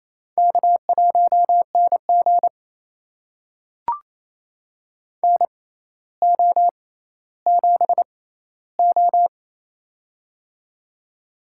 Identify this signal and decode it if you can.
Morse code, heard as a keyed tone, about 28 wpm, spaced as for 14 wpm, 700 Hz. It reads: K1NG NO7O
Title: USA Call Signs - 4 Characters - 6x Character Spacing - 28wpm